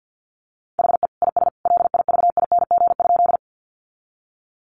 Morse code, heard as a keyed tone, about 50 wpm, 700 Hz. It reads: HE IS LIVING?
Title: Sentences from Top 1100 Words 50wpm